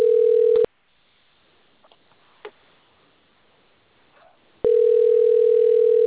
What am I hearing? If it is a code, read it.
Ringback tone (North America), 440 + 480 Hz.